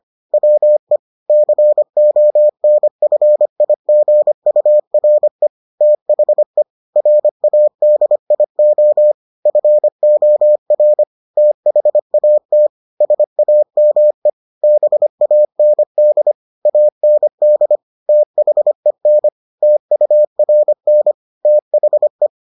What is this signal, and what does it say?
Morse code, heard as a keyed tone, about 25 wpm, 600 Hz. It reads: WE CONFIGURE THE RADIO FOR THAT SAME BAND AND THEN TURN THE